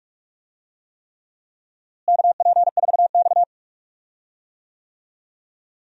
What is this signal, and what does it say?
Morse code, heard as a keyed tone, about 42 wpm, 700 Hz. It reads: KP4X
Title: DXCC-100 Call Signs - 4 Characters - Repeated 3x - 45wpm